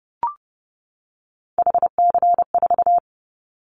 Morse code, heard as a keyed tone, about 30 wpm, 700 Hz. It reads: HC4
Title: Callsign Prefixes for Top 100 DXCC - 30wpm